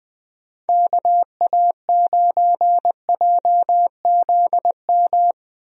Morse code, heard as a keyed tone, about 20 wpm, 700 Hz. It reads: KA9JZM
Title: USA Call Signs - 6 Characters - Repeated 2x - 20wpm